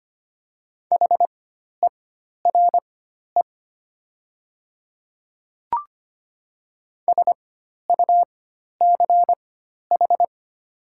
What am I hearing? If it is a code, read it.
Morse code, heard as a keyed tone, about 25 wpm, spaced as for 12 wpm, 700 Hz. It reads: HERE SUCH
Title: 4x Character Spacing - Top 500 Words - 25wpm